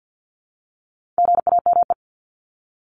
Morse code, heard as a keyed tone, about 50 wpm, 700 Hz. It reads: GAME
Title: Top 500 Words 50wpm